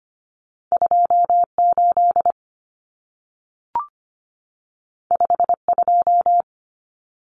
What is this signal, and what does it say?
Morse code, heard as a keyed tone, about 25 wpm, 700 Hz. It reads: 28 52
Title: Numbers - 2-Digit 25wpm